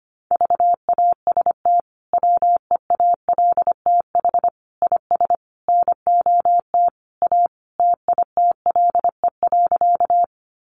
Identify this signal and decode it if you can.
Morse code, heard as a keyed tone, about 25 wpm, 700 Hz. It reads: VAST WEALTH IS NOT A TITLE.
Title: Sentences from Top 1300 Words 25wpm